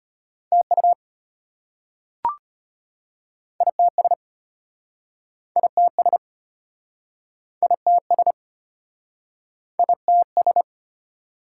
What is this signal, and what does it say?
Morse code, heard as a keyed tone, about 33 wpm, 700 Hz. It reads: TU ITS ITS ITS ITS